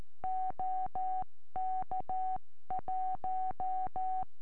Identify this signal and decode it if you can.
Morse code, heard as a keyed tone, about 13 wpm, 750 Hz. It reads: OK1
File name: eme / sm4ive 70cm